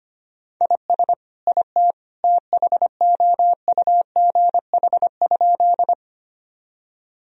Morse code, heard as a keyed tone, about 25 wpm, 700 Hz. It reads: IS IT THOUGH?
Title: Sentences from Top 1000 Words - Review Part 2 - 25wpm